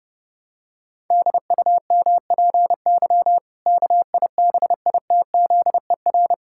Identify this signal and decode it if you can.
Morse code, heard as a keyed tone, about 30 wpm, 700 Hz. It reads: DUMPY KIBITZER
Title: Pangram Challenge v2 - 30wpm